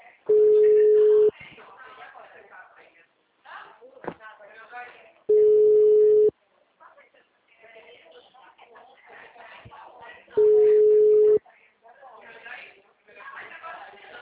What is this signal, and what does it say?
Ringback tone (Europe, 425 Hz), with speech.
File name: out- -2001- - - .62